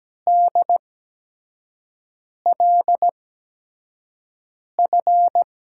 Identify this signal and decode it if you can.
Morse code, heard as a keyed tone, about 17 wpm, spaced as for 5 wpm, 700 Hz. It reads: DLF
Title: USA Call Signs - 6 Characters - 8x Character Spacing - 17wpm